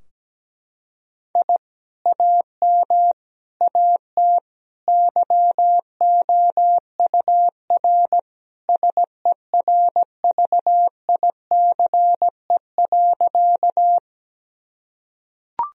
Morse code, heard as a keyed tone, about 17 wpm, 700 Hz. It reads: I AM AT YOUR SERVICE.
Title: Sentences from Top 1000 Words - Review Part 4 - 17wpm